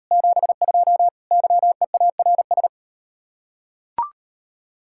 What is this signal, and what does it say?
Morse code, heard as a keyed tone, about 39 wpm, 700 Hz. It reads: 72 YEARS